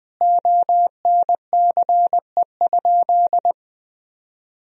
Morse code, heard as a keyed tone, about 20 wpm, 700 Hz. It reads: ONCE?
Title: Sentences from Top 1000 Words - Review Part 1 - 20wpm